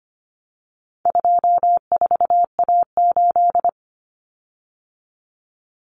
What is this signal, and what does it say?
Morse code, heard as a keyed tone, about 25 wpm, 700 Hz. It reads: 24A8
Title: Common Cut Numbers - 4-Digit 25wpm